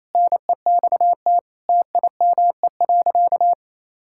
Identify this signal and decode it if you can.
Morse code, heard as a keyed tone, about 28 wpm, 700 Hz. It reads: NEXT TIME.
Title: Sentences from Top 1600 Words 28wpm